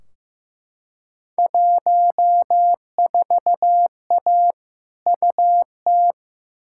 Morse code, heard as a keyed tone, about 15 wpm, 700 Hz. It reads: 14A UT
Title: Field Day Exchanges Encore - 15wpm